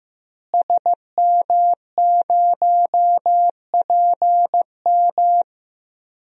Morse code, heard as a keyed tone, about 15 wpm, 700 Hz. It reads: SM0PM
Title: DXCC-100 Call Signs - 5 Characters - Repeated 2x - 15wpm